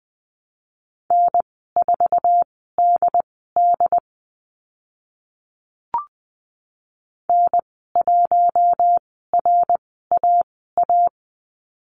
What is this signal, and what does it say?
Morse code, heard as a keyed tone, about 20 wpm, spaced as for 15 wpm, 700 Hz. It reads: N4DD N1RAA
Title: 076 Lesson 13g - Letter L - Call Signs 20wpm